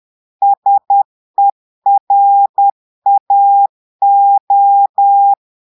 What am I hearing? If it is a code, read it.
Morse code, heard as a keyed tone, about 10 wpm, 800 Hz. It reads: SERAO
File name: portugais texte en clair 2 vitesse 10 mots minutes